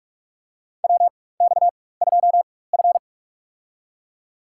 Morse code, heard as a keyed tone, about 46 wpm, spaced as for 21 wpm, 700 Hz. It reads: WX2F
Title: USA Call Signs - 4 Characters - 4x Character Spacing - 45wpm